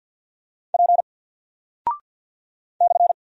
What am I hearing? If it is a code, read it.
Morse code, heard as a keyed tone, about 49 wpm, 700 Hz. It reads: P /
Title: Single Character (Letter:Number:Punctuation:Prosign) Alt 50wpm